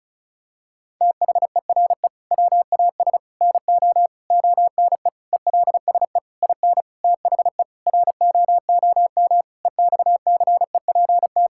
Morse code, heard as a keyed tone, about 36 wpm, 700 Hz. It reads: THERE WAS NO ONE ELSE IN THE ROOM EXCEPT
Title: Sentences from Top 1600 Words 35wpm